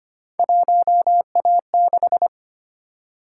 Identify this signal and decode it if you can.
Morse code, heard as a keyed tone, about 25 wpm, 700 Hz. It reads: 1A6
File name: Three Letter-Number Combinations - 25wpm